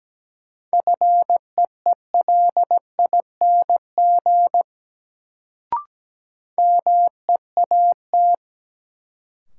Morse code, heard as a keyed tone, about 17 wpm, 700 Hz. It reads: FEELING MEAT
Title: Sentences from Top 1000 Words - Review Part 3 - New Word Drill - 17wpm